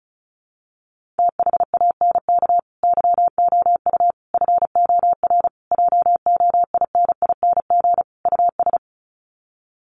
Morse code, heard as a keyed tone, about 34 wpm, 700 Hz. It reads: THANK YOU FOR JOINING US
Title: Sets of 5 Words 35wpm